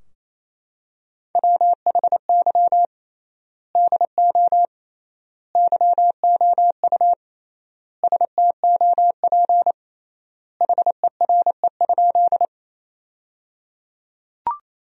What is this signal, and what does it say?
Morse code, heard as a keyed tone, about 28 wpm, 700 Hz. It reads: WHY DO YOU STOP HERE?
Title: 3x Word-Spacing - Sentences from Top 300 Words - 28wpm